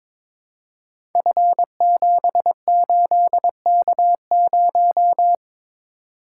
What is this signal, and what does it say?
Morse code, heard as a keyed tone, about 23 wpm, 700 Hz. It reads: F78K0
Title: Code Groups - 5 Letter:Numbers - 22wpm